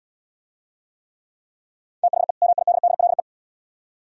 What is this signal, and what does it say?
Morse code, heard as a keyed tone, about 71 wpm, 700 Hz. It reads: THE MIDDLE